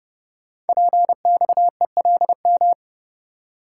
Morse code, heard as a keyed tone, about 30 wpm, 700 Hz. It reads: PXELM
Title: Code Groups - 5 Letters - 30wpm